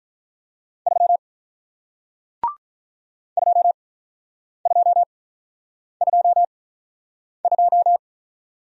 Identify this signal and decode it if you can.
Morse code, heard as a keyed tone, about 48 wpm, spaced as for 9 wpm, 700 Hz. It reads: 3 2222